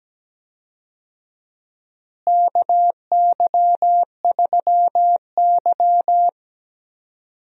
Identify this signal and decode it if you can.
Morse code, heard as a keyed tone, about 17 wpm, 700 Hz. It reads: KY3Y
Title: USA Call Signs - 4 Characters - Repeated 2x - 17wpm